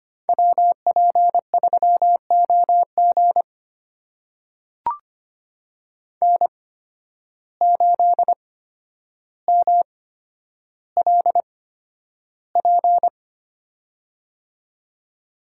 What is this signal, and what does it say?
Morse code, heard as a keyed tone, about 25 wpm, spaced as for 11 wpm, 700 Hz. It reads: WP3OG N8MLP
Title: USA Call Signs - 5 Characters - 8x Character Spacing - 25wpm